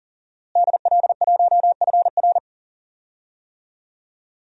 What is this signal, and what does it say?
Morse code, heard as a keyed tone, about 40 wpm, 700 Hz. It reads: DL1FR
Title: DXCC-100 Call Signs - 5 Characters - Repeated 2x - 40wpm